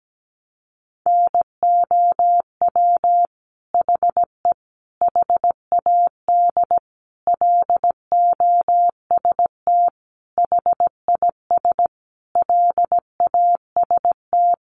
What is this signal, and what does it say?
Morse code, heard as a keyed tone, about 17 wpm, 700 Hz. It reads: NOW HE HAD LOST HIS LAST
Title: Sentences from Top 1400 Words 17wpm